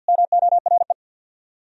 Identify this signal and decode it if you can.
Morse code, heard as a keyed tone, about 59 wpm, 700 Hz. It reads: MORE
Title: Speed-Racing - Sets of 4 Words 50wpm